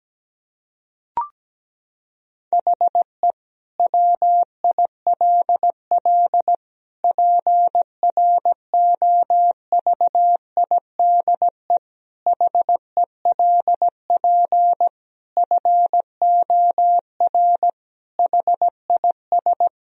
Morse code, heard as a keyed tone, about 17 wpm, 700 Hz. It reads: HE WILL PROVIDE HELP FOR HIS
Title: Sentences from Top 1000 Words - Review Part 3 - 17wpm